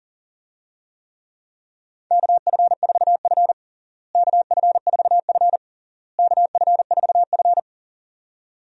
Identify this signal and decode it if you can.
Morse code, heard as a keyed tone, about 40 wpm, 700 Hz. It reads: KF4F KF4F KF4F
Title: USA Call Signs - 4 Characters - Repeated 3x - 40wpm